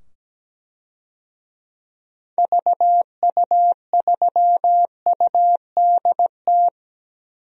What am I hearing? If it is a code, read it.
Morse code, heard as a keyed tone, about 17 wpm, 700 Hz. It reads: VU3UDT
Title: DXCC-100 Call Signs - 6 Characters - 17wpm